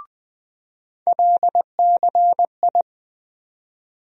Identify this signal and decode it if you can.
Morse code, heard as a keyed tone, about 20 wpm, 700 Hz. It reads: LCI
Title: USA Call Sign Postfixes - 3 Letters 20wpm